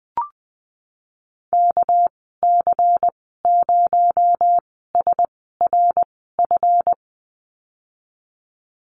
Morse code, heard as a keyed tone, about 20 wpm, spaced as for 15 wpm, 700 Hz. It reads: KC0SRF
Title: 238 Lesson 38g - Letter q - Call Signs 20wpm